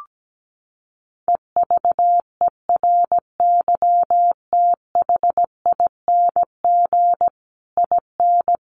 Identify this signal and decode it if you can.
Morse code, heard as a keyed tone, about 17 wpm, 700 Hz. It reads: EVERYTHING IN
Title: A Sound Of Thunder 17wpm